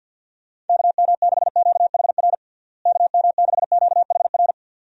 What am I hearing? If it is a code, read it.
Morse code, heard as a keyed tone, about 51 wpm, 700 Hz. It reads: KM6QHR KM6QHR
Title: USA Call Signs - 6 Characters - Repeated 2x - 50wpm